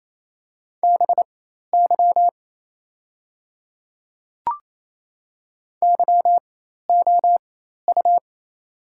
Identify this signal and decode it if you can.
Morse code, heard as a keyed tone, about 28 wpm, spaced as for 13 wpm, 700 Hz. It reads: BY YOU
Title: 4x Character Spacing - Top 200 Words - 28wpm